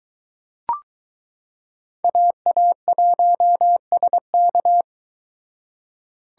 Morse code, heard as a keyed tone, about 22 wpm, 700 Hz. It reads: AA1SK